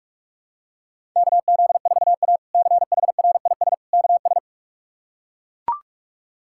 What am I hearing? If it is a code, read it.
Morse code, heard as a keyed tone, about 41 wpm, 700 Hz. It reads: KZ4A CHRIS KS
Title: K1USN SST QSO Practice - 45wpm